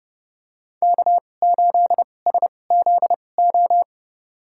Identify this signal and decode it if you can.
Morse code, heard as a keyed tone, about 30 wpm, spaced as for 22 wpm, 700 Hz. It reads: K8SZO
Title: USA Call Signs - 5 Characters - 2x Character Spacing - 30wpm